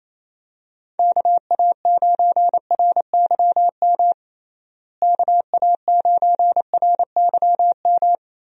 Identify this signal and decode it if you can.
Morse code, heard as a keyed tone, about 28 wpm, 700 Hz. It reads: KA9RYM KA9RYM
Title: USA Call Signs - 6 Characters - Repeated 3x - 28wpm